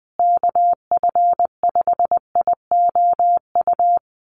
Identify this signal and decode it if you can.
Morse code, heard as a keyed tone, about 20 wpm, 700 Hz. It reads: KF5IOU
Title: USA Call Signs Repeated 2x 20wpm